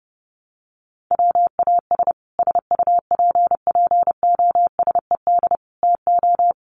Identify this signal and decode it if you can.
Morse code, heard as a keyed tone, about 30 wpm, 700 Hz. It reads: WAS SUPPOSED TO
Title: Sets of 4 Words 30wpm